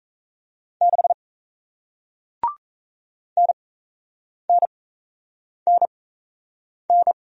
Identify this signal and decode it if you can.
Morse code, heard as a keyed tone, about 37 wpm, spaced as for 8 wpm, 700 Hz. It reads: 6 NNNN